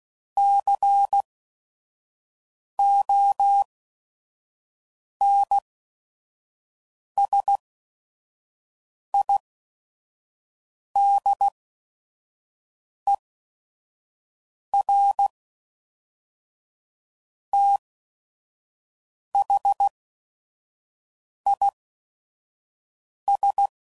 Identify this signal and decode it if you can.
Morse code, heard as a keyed tone, about 16 wpm, spaced as for 5 wpm, 800 Hz. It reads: CONSIDER THIS
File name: CW Code Groups @5WPM / Technical Subjects / AND Gate